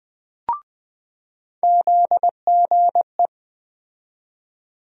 Morse code, heard as a keyed tone, about 21 wpm, 700 Hz. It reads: ZGE
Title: USA Call Sign Postfixes - 3 Letters 20wpm